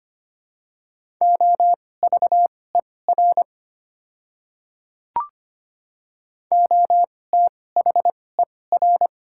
Morse code, heard as a keyed tone, about 25 wpm, spaced as for 18 wpm, 700 Hz. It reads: OVER OTHER
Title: 2x Character Spacing - Top 200 Words - 25wpm